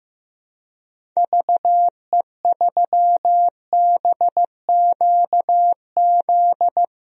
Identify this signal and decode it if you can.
Morse code, heard as a keyed tone, about 15 wpm, 700 Hz. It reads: VE3BQZ
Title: DXCC-100 Call Signs - 6 Characters - Repeated 3x - 15wpm